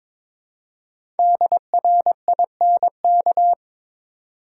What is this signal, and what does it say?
Morse code, heard as a keyed tone, about 22 wpm, 690 Hz. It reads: DRINK
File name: 5 Letter Words - First 3 Letters Given - 22wpm